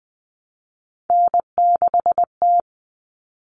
Morse code, heard as a keyed tone, about 20 wpm, 700 Hz. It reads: N6T
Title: All Cut Numbers - 3-Digit 20wpm